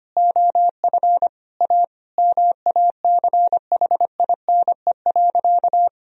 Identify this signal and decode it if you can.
Morse code, heard as a keyed tone, about 25 wpm, 700 Hz. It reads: OF A MACHINE.
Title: Sentences from Top 1000 Words - Review Part 3 - 25wpm